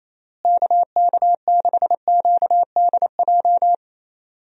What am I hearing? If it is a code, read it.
Morse code, heard as a keyed tone, about 28 wpm, 700 Hz. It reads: KK6QDJ
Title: USA Call Signs - 6 Characters - Repeated 3x - 28wpm